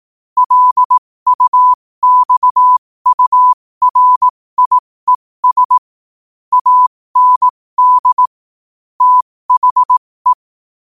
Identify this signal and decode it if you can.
Morse code, heard as a keyed tone, about 18 wpm, spaced as for 15 wpm, 1000 Hz. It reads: LUXURIES AND THE
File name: qotdfiles / 15wpm